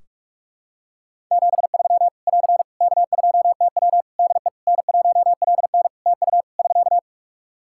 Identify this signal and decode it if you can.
Morse code, heard as a keyed tone, about 46 wpm, 700 Hz. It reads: 73 <AR> K2TW DE N1LN TU <SK>
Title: Practice QSO - Exchange 4 - Ending - 45wpm